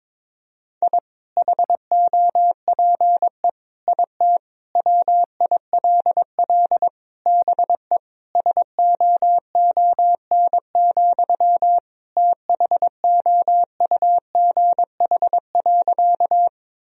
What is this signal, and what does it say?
Morse code, heard as a keyed tone, about 22 wpm, 700 Hz. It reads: I HOPE IT WILL BE SOON, THOUGH.
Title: Sentences from Top 1000 Words - Review Part 2 - 22wpm